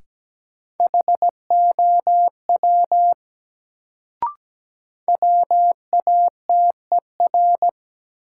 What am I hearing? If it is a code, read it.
Morse code, heard as a keyed tone, about 17 wpm, 700 Hz. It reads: HOW WATER
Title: Sentences from Top 100 Words - New Word Drill - 17wpm